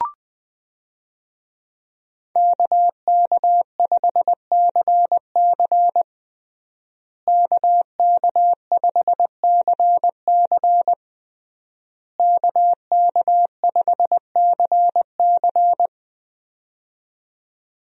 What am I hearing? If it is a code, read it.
Morse code, heard as a keyed tone, about 20 wpm, 700 Hz. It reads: KK5CC KK5CC KK5CC
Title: USA Call Signs - 5 Characters - Repeated 3x - 20wpm